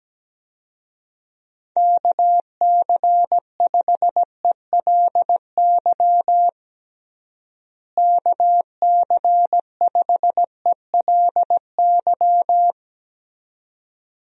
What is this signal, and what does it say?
Morse code, heard as a keyed tone, about 17 wpm, 700 Hz. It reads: KC5ELY KC5ELY
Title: USA Call Signs - 6 Characters - Repeated 2x - 17wpm